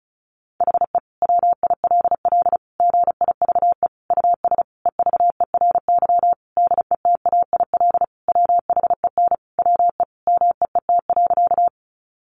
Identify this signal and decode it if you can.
Morse code, heard as a keyed tone, about 34 wpm, 700 Hz. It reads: HE WILL GIVE US EVERY DETAIL WHEN WE MEET.
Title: Sentences from Top 1500 Words 35wpm